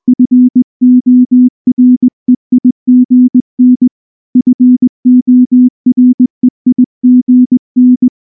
Morse code, heard as a keyed tone, about 20 wpm, 250 Hz. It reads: FOREIGN FOREIGN
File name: morse-training-20WPM